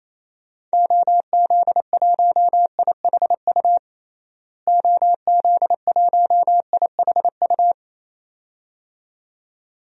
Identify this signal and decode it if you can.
Morse code, heard as a keyed tone, about 28 wpm, 700 Hz. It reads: OZ1IHU OZ1IHU
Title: DXCC-100 Call Signs - 6 Characters - Repeated 2x - 28wpm